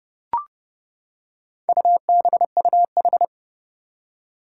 Morse code, heard as a keyed tone, about 30 wpm, 700 Hz. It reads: UBUH
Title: Four Challenging Letter-Number Combinations 30wpm